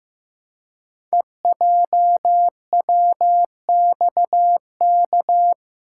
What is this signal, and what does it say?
Morse code, heard as a keyed tone, about 15 wpm, 700 Hz. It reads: EJWXK
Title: Code Groups - 5 Letters - 15wpm